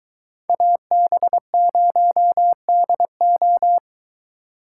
Morse code, heard as a keyed tone, about 22 wpm, 700 Hz. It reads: AB0DO